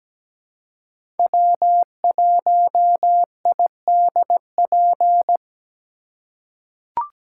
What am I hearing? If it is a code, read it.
Morse code, heard as a keyed tone, about 17 wpm, 700 Hz. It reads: W1IDP
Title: USA Call Signs - 2x Character Spacing - 17wpm